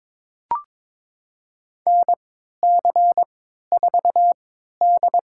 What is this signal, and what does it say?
Morse code, heard as a keyed tone, about 22 wpm, spaced as for 13 wpm, 700 Hz. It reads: NC4D
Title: USA Call Signs - 4 Characters - 3x Character Spacing - 22wpm